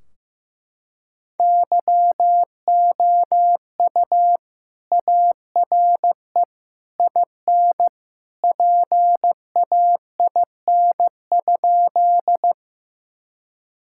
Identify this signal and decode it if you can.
Morse code, heard as a keyed tone, about 15 wpm, 700 Hz. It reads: YOU ARE IN PAIN?
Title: Sentences from Top 1600 Words 15wpm